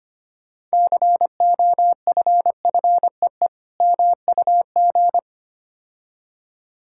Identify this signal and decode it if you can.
Morse code, heard as a keyed tone, about 27 wpm, 700 Hz. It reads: COFFEE MUG
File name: Easy Words - Speed-Racing with Hints - 25wpm